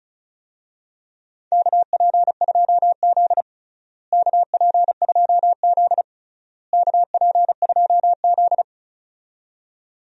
Morse code, heard as a keyed tone, about 36 wpm, 700 Hz. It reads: KP2Z KP2Z KP2Z
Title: DXCC-100 Call Signs - 4 Characters - Repeated 3x - 35wpm